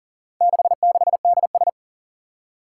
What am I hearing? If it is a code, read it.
Morse code, heard as a keyed tone, about 39 wpm, 700 Hz. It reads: 66DS
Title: Four Challenging Letter-Number Combinations 40wpm